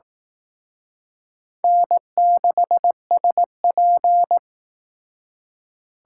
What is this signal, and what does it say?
Morse code, heard as a keyed tone, about 19 wpm, 700 Hz. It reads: N6SP